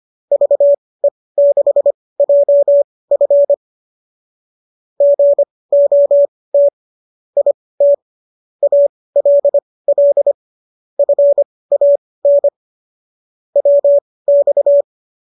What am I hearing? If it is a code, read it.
Morse code, heard as a keyed tone, about 25 wpm, spaced as for 18 wpm, 570 Hz. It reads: VE6JF GOT IT ALL FAN WX